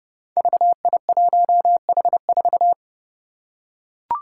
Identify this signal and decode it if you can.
Morse code, heard as a keyed tone, about 30 wpm, 700 Hz. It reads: VI1H4
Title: Five Challenging Letter-Number Combinations 30wpm